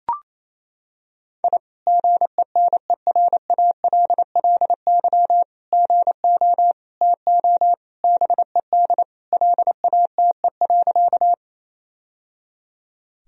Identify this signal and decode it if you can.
Morse code, heard as a keyed tone, about 28 wpm, 700 Hz. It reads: I GENERALLY GO TO BED LATE.
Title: Sentences from Top 1000 Words - Review Part 4 - 28wpm